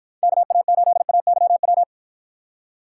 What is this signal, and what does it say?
Morse code, heard as a keyed tone, about 52 wpm, 700 Hz. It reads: KA8AYW